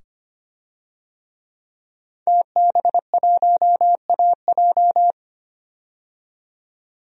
Morse code, heard as a keyed tone, about 25 wpm, 700 Hz. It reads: TB1AJ
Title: DXCC-100 Call Signs - 5 Characters - 25wpm